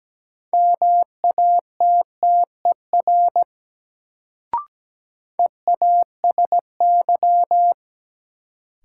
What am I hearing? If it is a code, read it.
Morse code, heard as a keyed tone, about 17 wpm, 700 Hz. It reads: MATTER EASY
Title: Sentences from Top 700 Words - New Word Drill - 17wpm